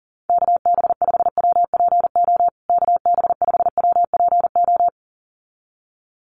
Morse code, heard as a keyed tone, about 37 wpm, 700 Hz. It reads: KB5WPO KB5WPO
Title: USA Call Signs Repeated 2x 40wpm